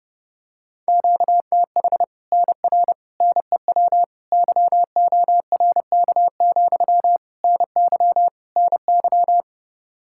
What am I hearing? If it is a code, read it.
Morse code, heard as a keyed tone, about 32 wpm, 700 Hz. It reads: QTH NR NEW YORK, NY NY
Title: Practice QSO - Exchange 1 - Beginning - 30wpm